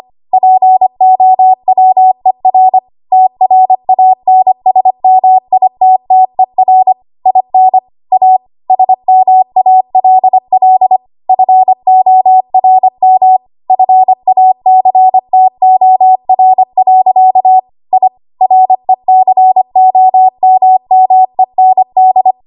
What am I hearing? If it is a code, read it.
Morse code, heard as a keyed tone, about 25 wpm, 750 Hz. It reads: POWER TRANSMITTER IN A SMALL FORM FACTOR. I RECOMMEND
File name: Archive / 25 WPM / 25WPM